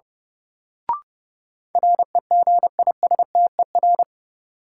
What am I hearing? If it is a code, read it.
Morse code, heard as a keyed tone, about 30 wpm, 700 Hz. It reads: REGISTER
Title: Sentences from Top 1200 Words - New Word Drill - 30wpm